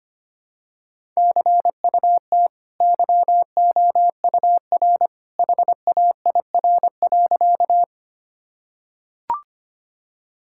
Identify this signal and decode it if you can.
Morse code, heard as a keyed tone, about 25 wpm, 700 Hz. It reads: CUT YOUR HAIR.
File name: Easy Sentences with Hints - 25wpm